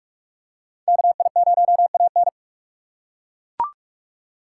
Morse code, heard as a keyed tone, about 42 wpm, 700 Hz. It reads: KI0AN
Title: USA Call Signs - 5 Characters - 6x Character Spacing - 45wpm